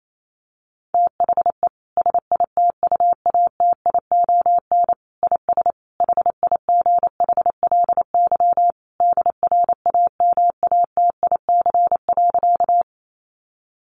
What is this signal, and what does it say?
Morse code, heard as a keyed tone, about 28 wpm, 700 Hz. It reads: THE SITUATION IS HIGHLY DRAMATIC.
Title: Sentences from Top 1300 Words 28wpm